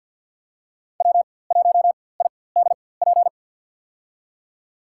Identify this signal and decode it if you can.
Morse code, heard as a keyed tone, about 51 wpm, spaced as for 23 wpm, 700 Hz. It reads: W1IDP
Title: USA Call Signs - 5 Characters - 4x Character Spacing - 50wpm